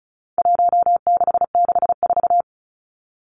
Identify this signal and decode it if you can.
Morse code, heard as a keyed tone, about 34 wpm, 700 Hz. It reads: 1664
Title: Numbers - Contest Style - 4-Digit 35wpm